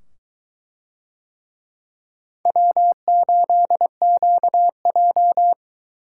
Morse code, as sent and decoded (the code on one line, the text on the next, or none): .-- ---.. --.- .---
W8QJ